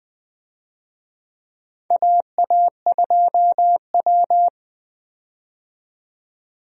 Morse code, heard as a keyed tone, about 20 wpm, 700 Hz. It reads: AA2W
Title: USA Call Signs - 4 Characters - 20wpm